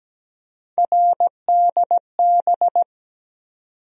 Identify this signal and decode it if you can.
Morse code, heard as a keyed tone, about 17 wpm, 700 Hz. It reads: RDB
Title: USA Call Sign Postfixes - 3 Letters 17wpm